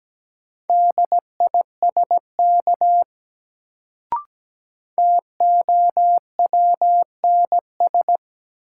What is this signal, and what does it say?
Morse code, heard as a keyed tone, about 17 wpm, 700 Hz. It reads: DISK TOWNS
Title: Sentences from Top 1000 Words - Review Part 6 - New Word Drill - 17wpm